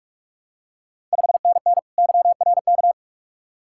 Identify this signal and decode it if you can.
Morse code, heard as a keyed tone, about 46 wpm, 690 Hz. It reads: 5NN YRK
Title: Ontario QSO Party Exchange 45wpm